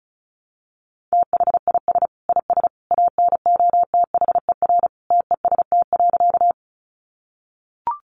Morse code, heard as a keyed tone, about 34 wpm, 700 Hz. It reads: THIS IS ANOTHER TEST.
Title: Sentences from Top 600 Words 35wpm